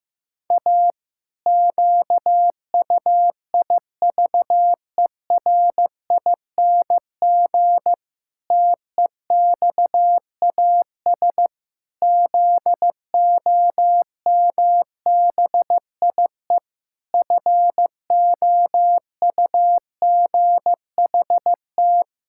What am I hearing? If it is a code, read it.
Morse code, heard as a keyed tone, about 15 wpm, 700 Hz. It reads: A QUIVERING TEXAS ZOMBIE FOUGHT
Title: Pangram Challenge v2 - 15wpm